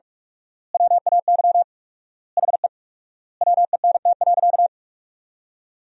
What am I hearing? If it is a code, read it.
Morse code, heard as a keyed tone, about 42 wpm, 700 Hz. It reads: WAY HE WENT.
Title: 4x Word-Spacing - Sentences from Top 200 Words - 45wpm